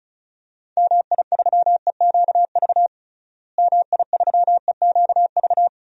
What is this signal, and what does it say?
Morse code, heard as a keyed tone, about 34 wpm, 700 Hz. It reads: MI3EQV MI3EQV
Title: DXCC-100 Call Signs - 6 Characters - Repeated 3x - 35wpm